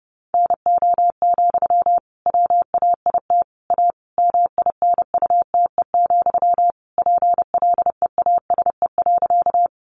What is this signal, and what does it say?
Morse code, heard as a keyed tone, about 30 wpm, 700 Hz. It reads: NO, WAIT A MINUTE, PLEASE.
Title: Sentences from Top 1500 Words 30wpm